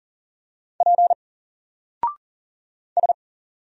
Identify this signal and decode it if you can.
Morse code, heard as a keyed tone, about 39 wpm, 700 Hz. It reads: P S
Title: Single Character (Letter:Number:Punctuation:Prosign) 40wpm